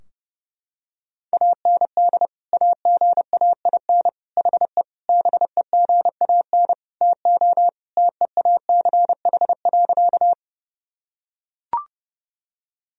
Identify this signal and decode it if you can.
Morse code, heard as a keyed tone, about 30 wpm, 700 Hz. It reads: AND AGAIN HE BEGAN TO TEACH.
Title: Sentences from Top 1000 Words - Review Part 2 - 30wpm